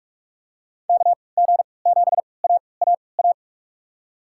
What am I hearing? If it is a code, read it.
Morse code, heard as a keyed tone, about 43 wpm, spaced as for 25 wpm, 700 Hz. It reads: KG7AAA
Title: USA Call Signs - 6 Characters - 3x Character Spacing - 45wpm